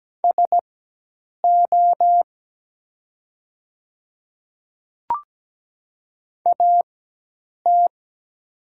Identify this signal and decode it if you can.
Morse code, heard as a keyed tone, about 17 wpm, spaced as for 8 wpm, 700 Hz. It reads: SO AT
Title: 4x Character Spacing - Top 100 Words - 17wpm